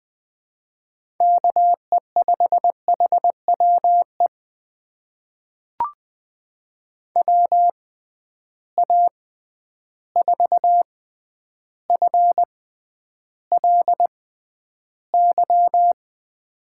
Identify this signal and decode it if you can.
Morse code, heard as a keyed tone, about 20 wpm, spaced as for 10 wpm, 700 Hz. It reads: KE5HWE WA4FLY
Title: USA Call Signs - 6 Characters - 6x Character Spacing - 20wpm